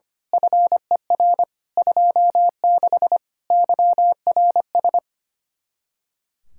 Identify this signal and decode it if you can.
Morse code, heard as a keyed tone, about 24 wpm, 700 Hz. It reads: FER 26 YRS
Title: Speed-Racing Common QSO Phrases Encore - 25wpm